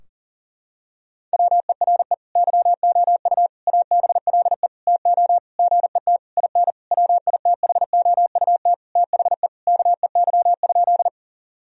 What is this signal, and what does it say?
Morse code, heard as a keyed tone, about 40 wpm, 700 Hz. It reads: WERE YOU ABLE TO GET IN WITHOUT THE KEY?
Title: Sentences from Top 1000 Words - Review Part 4 - 40wpm